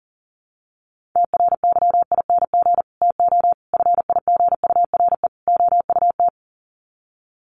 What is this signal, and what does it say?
Morse code, heard as a keyed tone, about 40 wpm, 700 Hz. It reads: TRYING TO FIGURE OUT
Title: Sets of 4 Words 40wpm